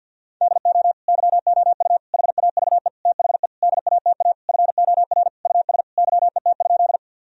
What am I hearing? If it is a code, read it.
Morse code, heard as a keyed tone, about 51 wpm, 700 Hz. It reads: DO YOU HAVE THE DATA FOR US YET?
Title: Sentences from Top 1000 Words - Review Part 3 - 50wpm